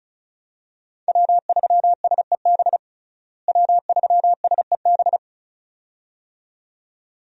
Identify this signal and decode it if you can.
Morse code, heard as a keyed tone, about 36 wpm, 700 Hz. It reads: W3SEB W3SEB
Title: USA Call Signs - 5 Characters - Repeated 2x - 35wpm